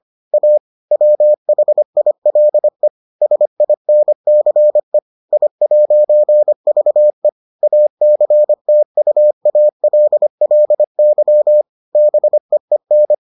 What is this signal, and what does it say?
Morse code, heard as a keyed tone, about 25 wpm, 600 Hz. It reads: A WHILE SINCE I'VE ACTUALLY BEEN